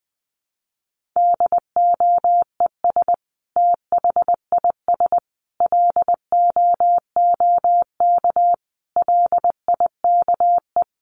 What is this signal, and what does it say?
Morse code, heard as a keyed tone, about 20 wpm, 700 Hz. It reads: DOES THIS LOOK LIKE
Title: Sentences from Top 1500 Words 20wpm